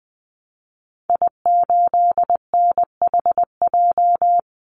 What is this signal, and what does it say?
Morse code, heard as a keyed tone, about 20 wpm, 700 Hz. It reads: I8NHJ
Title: Call Signs - 20wpm